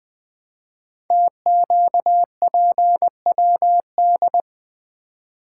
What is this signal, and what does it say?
Morse code, heard as a keyed tone, about 20 wpm, 700 Hz. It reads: TQPWD
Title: Code Groups - 5 Letters - 20wpm